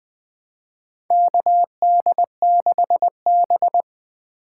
Morse code, heard as a keyed tone, about 20 wpm, 700 Hz. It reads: KD6B
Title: USA Call Signs - 4 Characters - 2x Character Spacing - 20wpm